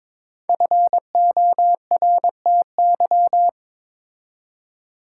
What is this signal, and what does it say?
Morse code, heard as a keyed tone, about 22 wpm, 700 Hz. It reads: FORTY
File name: 5 Letter Words - First 3 Letters Given - 22wpm